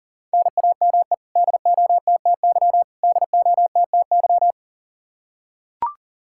Morse code, heard as a keyed tone, about 40 wpm, 700 Hz. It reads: NAME DOTTY DOTTY
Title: Practice QSO - Exchange 1 - Beginning - 40wpm